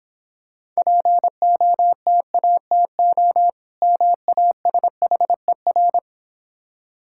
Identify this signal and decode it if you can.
Morse code, heard as a keyed tone, about 25 wpm, 700 Hz. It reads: POTATO MASHER
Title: Easy Words - Speed-Racing with Hints - 20wpm